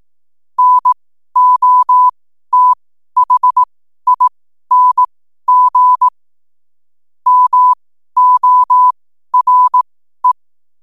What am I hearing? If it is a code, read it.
Morse code, heard as a keyed tone, about 18 wpm, spaced as for 12 wpm, 1000 Hz. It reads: NOTHING MORE
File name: n7net / 12wpm